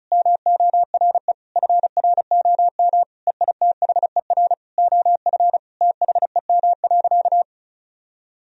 Morse code, heard as a keyed tone, about 36 wpm, 700 Hz. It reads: MORE FROM EITHER OF THEM.
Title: Sentences from Top 1000 Words - Review Part 4 - 35wpm